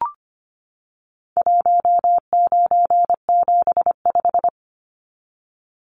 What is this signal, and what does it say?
Morse code, heard as a keyed tone, about 25 wpm, 700 Hz. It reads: 1975
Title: Numbers - 4-Digit 25wpm